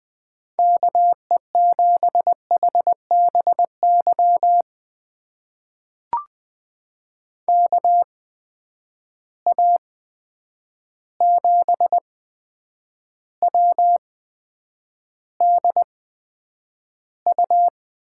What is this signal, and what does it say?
Morse code, heard as a keyed tone, about 20 wpm, spaced as for 9 wpm, 700 Hz. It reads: KE7HBY KA7WDU
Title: USA Call Signs - 6 Characters - 8x Character Spacing - 20wpm